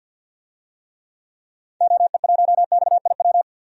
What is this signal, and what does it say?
Morse code, heard as a keyed tone, about 50 wpm, 700 Hz. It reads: OE1XIW
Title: DXCC-100 Call Signs - 6 Characters - 50wpm